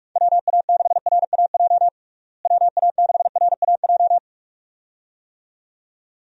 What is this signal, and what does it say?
Morse code, heard as a keyed tone, about 41 wpm, 700 Hz. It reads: WA6RAJ WA6RAJ
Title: USA Call Signs - 6 Characters - Repeated 2x - 45wpm